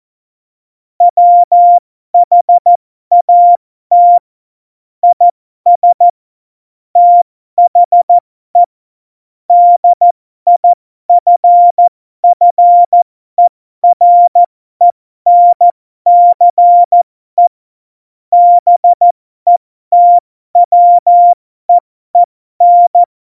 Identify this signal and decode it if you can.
Morse code, heard as a keyed tone, about 14 wpm, spaced as for 12 wpm, 700 Hz. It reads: WHAT IS THE DIFFERENCE BETWEEN
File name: Morsekurs / Witze